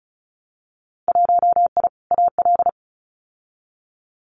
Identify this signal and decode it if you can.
Morse code, heard as a keyed tone, about 34 wpm, 700 Hz. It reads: 1I AL
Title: Winter Field Day Exchanges 35wpm